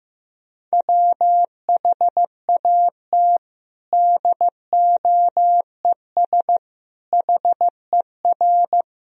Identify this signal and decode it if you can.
Morse code, heard as a keyed tone, about 15 wpm, 700 Hz. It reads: WHAT DOES HER
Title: Sentences from Top 1000 Words - Review Part 4 - 15wpm